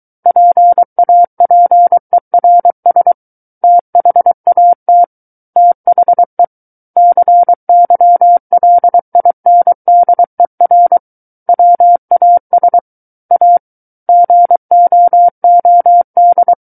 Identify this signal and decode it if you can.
Morse code, heard as a keyed tone, about 23 wpm, 700 Hz. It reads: PAPERS THAT THE CYLINDER WAS A GOOD
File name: Book One / 14-Chapter 14 - 23 WPM